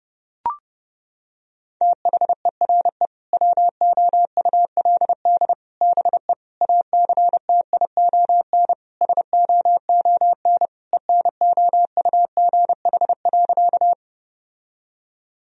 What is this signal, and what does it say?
Morse code, heard as a keyed tone, about 30 wpm, 700 Hz. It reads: THERE WOULD BE ACTION SOON ENOUGH.
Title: Sentences from Top 1000 Words - Review Part 4 - 30wpm